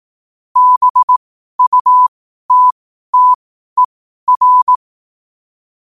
Morse code, heard as a keyed tone, about 18 wpm, spaced as for 12 wpm, 1000 Hz. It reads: BUTTER
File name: qotdfiles / 12wpm